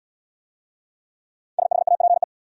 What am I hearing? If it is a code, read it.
Morse code, heard as a keyed tone, about 71 wpm, 700 Hz. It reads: SHAPE